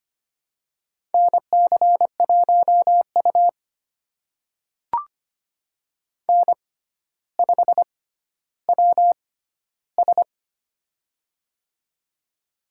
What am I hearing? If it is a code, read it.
Morse code, heard as a keyed tone, about 25 wpm, spaced as for 13 wpm, 700 Hz. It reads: NC1U N5WS
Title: USA Call Signs - 6x Character Spacing - 25wpm